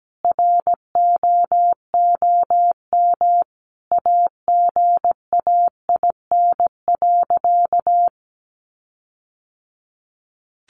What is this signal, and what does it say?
Morse code, heard as a keyed tone, about 17 wpm, 700 Hz. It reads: ROOM AGAIN.
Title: Sentences from Top 1300 Words 17wpm